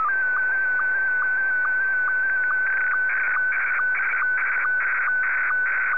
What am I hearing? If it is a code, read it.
SSTV (slow-scan television), Scottie 1.